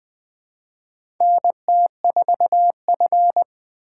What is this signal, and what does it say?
Morse code, heard as a keyed tone, about 20 wpm, 700 Hz. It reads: NT4F
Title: USA Call Signs - 4 Characters - 2x Character Spacing - 20wpm